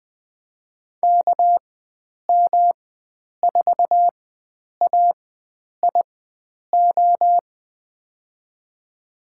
Morse code, heard as a keyed tone, about 20 wpm, spaced as for 9 wpm, 700 Hz. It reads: KM4AIO
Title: USA Call Signs - 4x Character Spacing - 20wpm